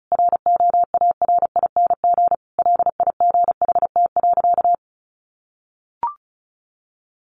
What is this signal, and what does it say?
Morse code, heard as a keyed tone, about 34 wpm, 700 Hz. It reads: ROARING LIGHT.
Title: A Sound Of Thunder 35wpm